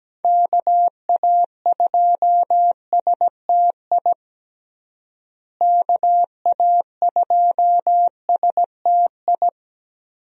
Morse code, heard as a keyed tone, about 17 wpm, 700 Hz. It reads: KA2STI KA2STI
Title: USA Call Signs - 6 Characters - Repeated 2x - 17wpm